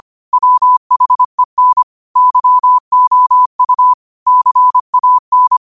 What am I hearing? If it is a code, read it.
Morse code, heard as a keyed tone, about 25 wpm, 1000 Hz. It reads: WHEN YOU CAN
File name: qotdfiles / 25wpm